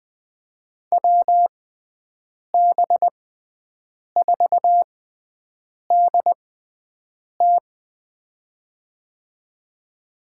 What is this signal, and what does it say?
Morse code, heard as a keyed tone, about 20 wpm, spaced as for 7 wpm, 700 Hz. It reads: WB4DT
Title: USA Call Signs - 5 Characters - 6x Character Spacing - 20wpm